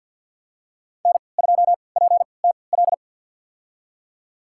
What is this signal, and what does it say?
Morse code, heard as a keyed tone, about 48 wpm, spaced as for 28 wpm, 700 Hz. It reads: N2PTL
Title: USA Call Signs - 5 Characters - 3x Character Spacing - 50wpm